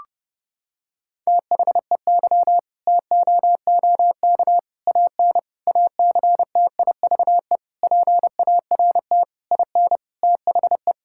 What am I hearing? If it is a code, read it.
Morse code, heard as a keyed tone, about 30 wpm, 700 Hz. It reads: THEY TOOK AN ACTIVE PART IN THE
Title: Sentences from Top 1000 Words - Review Part 6 - 30wpm